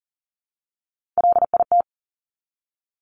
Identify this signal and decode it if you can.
Morse code, heard as a keyed tone, about 42 wpm, 700 Hz. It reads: LIT
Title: Top Three Letter Combinations in Words 40wpm